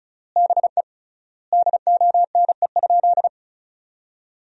Morse code, heard as a keyed tone, about 34 wpm, 700 Hz. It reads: BE DONE?
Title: 3x Word-Spacing - Sentences from Top 500 Words - 35wpm